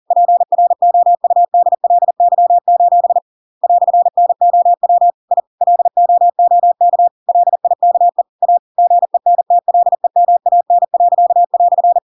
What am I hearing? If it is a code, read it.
Morse code, heard as a keyed tone, about 40 wpm, 700 Hz. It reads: PROUDLY: "NOW I LOOK LIKE A GENTLEMAN."
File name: Pinocchio / Pinocchio ch8 40 40 ik1hge